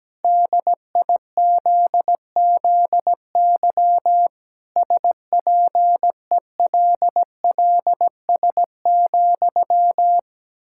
Morse code, heard as a keyed tone, about 17 wpm, 700 Hz. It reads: DIZZY SPELLS,
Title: Pangram Challenge v2 - 17wpm